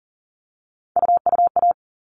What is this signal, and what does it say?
Morse code, heard as a keyed tone, about 40 wpm, 700 Hz. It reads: UUA
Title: All Cut Numbers - 3-Digit 40wpm